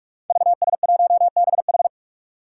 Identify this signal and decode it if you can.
Morse code, heard as a keyed tone, about 44 wpm, 700 Hz. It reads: VS1BH
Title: Five Challenging Letter-Number Combinations 45wpm